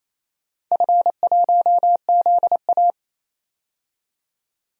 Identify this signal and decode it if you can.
Morse code, heard as a keyed tone, about 28 wpm, 700 Hz. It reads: F1ZA
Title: DXCC-100 Call Signs - 4 Characters - 28wpm